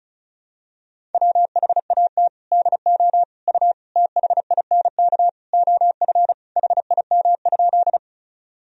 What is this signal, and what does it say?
Morse code, heard as a keyed tone, about 34 wpm, 700 Hz. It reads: WHAT DO U THINK OF HIM?
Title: Sentences from Top 200 Words and Abbreviations 35wpm